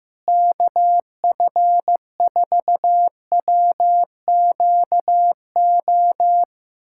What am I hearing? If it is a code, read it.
Morse code, heard as a keyed tone, about 15 wpm, 700 Hz. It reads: KF4WQO
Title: USA Callsign DE Callsign - 6 Characters - 15wpm